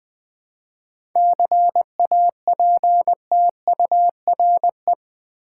Morse code, heard as a keyed tone, about 20 wpm, 700 Hz. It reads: CAPTURE
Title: 7 Letter Words - First 3 Letters Given - 20wpm